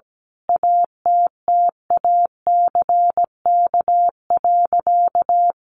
Morse code, heard as a keyed tone, about 17 wpm, 700 Hz. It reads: ATTACK.
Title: Sentences from Top 1500 Words 17wpm